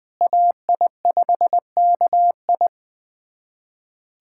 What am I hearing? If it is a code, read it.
Morse code, heard as a keyed tone, about 20 wpm, 700 Hz. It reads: AI5KI
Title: USA Callsign DE Callsign - 5 Characters - 20wpm